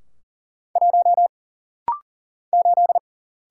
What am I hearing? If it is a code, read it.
Morse code, heard as a keyed tone, about 39 wpm, 700 Hz. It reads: 1 8
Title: Single Letter-Number - Rapid-Fire 40wpm